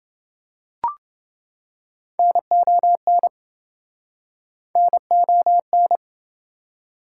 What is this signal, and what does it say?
Morse code, heard as a keyed tone, about 26 wpm, 700 Hz. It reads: NON NON